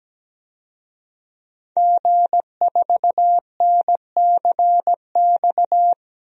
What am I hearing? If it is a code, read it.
Morse code, heard as a keyed tone, about 17 wpm, 700 Hz. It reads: G4NCX
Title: DXCC-100 Call Signs - 5 Characters - Repeated 2x - 17wpm